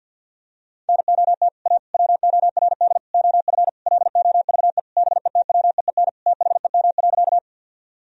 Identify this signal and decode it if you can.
Morse code, heard as a keyed tone, about 52 wpm, 700 Hz. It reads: NOT A WORD OF LOVE BETWEEN THEM.
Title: Sentences from Top 1000 Words - Review Part 1 - 50wpm